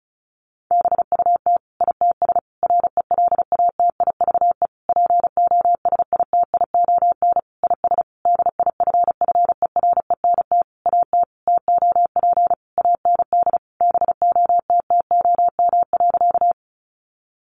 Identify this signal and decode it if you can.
Morse code, heard as a keyed tone, about 34 wpm, 700 Hz. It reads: BUT ITS RELATIVE POSITION IS DIFFERENT AT TOP AND BOTTOM.
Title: Sentences from Top 1400 Words 35wpm